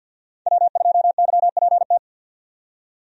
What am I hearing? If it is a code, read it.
Morse code, heard as a keyed tone, about 52 wpm, 700 Hz. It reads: W2YPT
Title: USA Call Signs - 5 Characters - 50wpm